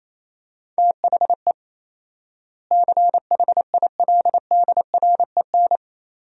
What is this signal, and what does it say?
Morse code, heard as a keyed tone, about 28 wpm, 700 Hz. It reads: THE CHILDREN
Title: 4x Word-Spacing - Sentences from Top 300 Words - 28wpm